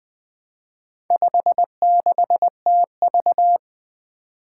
Morse code, heard as a keyed tone, about 20 wpm, 700 Hz. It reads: 56TV
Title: Four Challenging Letter-Number Combinations 20wpm